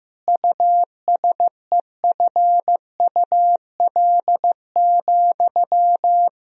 Morse code, heard as a keyed tone, about 15 wpm, 700 Hz. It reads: USEFUL,
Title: Sentences from Top 1000 Words - Review Part 4 - 15wpm